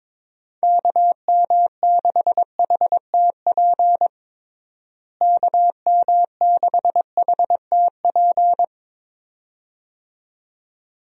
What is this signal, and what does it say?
Morse code, heard as a keyed tone, about 21 wpm, 700 Hz. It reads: KM6HTP KM6HTP
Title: USA Call Signs - 6 Characters - Repeated 2x - 22wpm